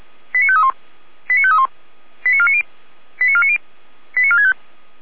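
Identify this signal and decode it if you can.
Five-tone selective call (ZVEI), heard as quick runs of five tones, tones of about 70 ms. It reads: 89531 89531 89490 89490 89465
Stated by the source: ZVEI five-tone call received by SDR.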